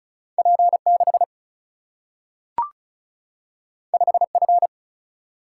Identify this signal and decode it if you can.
Morse code, heard as a keyed tone, about 36 wpm, 700 Hz. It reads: P6 5F
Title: Single-Letter Single-Number 35wpm